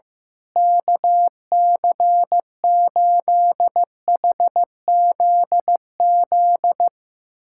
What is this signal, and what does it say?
Morse code, heard as a keyed tone, about 15 wpm, 700 Hz. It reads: KC8HZZ
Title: USA Callsign DE Callsign - 6 Characters - 15wpm